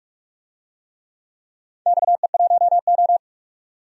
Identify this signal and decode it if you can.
Morse code, heard as a keyed tone, about 47 wpm, 700 Hz. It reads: XE1O
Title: DXCC-100 Call Signs - 4 Characters - 45wpm